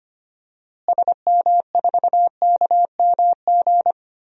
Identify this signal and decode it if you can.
Morse code, heard as a keyed tone, about 25 wpm, 700 Hz. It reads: SM4KMG
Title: DXCC-100 Call Signs - 6 Characters - Repeated 3x - 25wpm